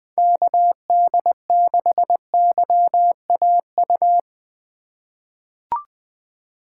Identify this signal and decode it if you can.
Morse code, heard as a keyed tone, about 20 wpm, 700 Hz. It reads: KD6YAU
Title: USA Call Signs - 6 Characters - 2x Character Spacing - 20wpm